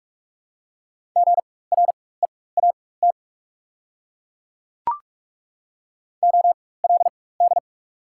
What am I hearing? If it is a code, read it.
Morse code, heard as a keyed tone, about 42 wpm, spaced as for 20 wpm, 690 Hz. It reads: GREAT OLD
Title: 4x Character Spacing - Top 200 Words - 45wpm